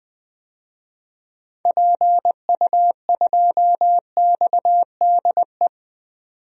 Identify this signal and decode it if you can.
Morse code, heard as a keyed tone, about 20 wpm, 700 Hz. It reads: PU2XDE
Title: DXCC-100 Call Signs - 6 Characters - 20wpm